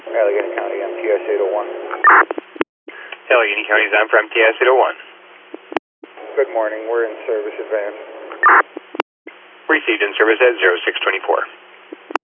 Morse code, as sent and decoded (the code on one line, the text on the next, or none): none